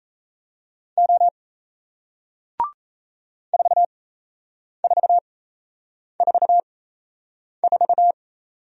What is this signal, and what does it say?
Morse code, heard as a keyed tone, about 39 wpm, spaced as for 8 wpm, 700 Hz. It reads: O 4444